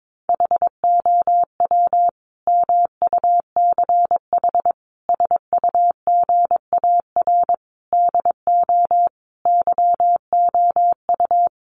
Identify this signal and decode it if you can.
Morse code, heard as a keyed tone, about 22 wpm, 700 Hz. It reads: HOW MUCH SUGAR DO YOU
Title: Sentences from Top 1200 Words 22wpm